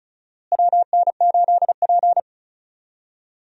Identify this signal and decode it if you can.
Morse code, heard as a keyed tone, about 36 wpm, 700 Hz. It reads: WN8P
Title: USA Call Signs - 4 Characters - Repeated 3x - 35wpm